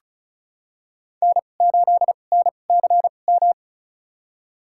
Morse code, heard as a keyed tone, about 36 wpm, spaced as for 25 wpm, 700 Hz. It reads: N8NCM
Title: USA Call Signs - 2x Character Spacing - 35wpm